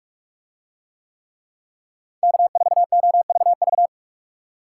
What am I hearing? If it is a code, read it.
Morse code, heard as a keyed tone, about 45 wpm, 700 Hz. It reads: K4OVV